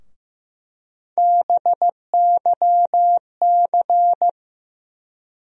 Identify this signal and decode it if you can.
Morse code, heard as a keyed tone, about 15 wpm, 700 Hz. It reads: BYC
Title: USA Call Sign Postfixes - 3 Letters 15wpm